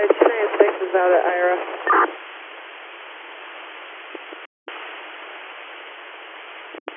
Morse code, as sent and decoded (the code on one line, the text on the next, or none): none